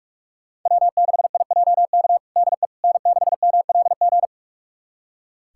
Morse code, heard as a keyed tone, about 41 wpm, 700 Hz. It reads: W6IJK DE N6MLG
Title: USA Callsign DE Callsign - 5 Characters - 45wpm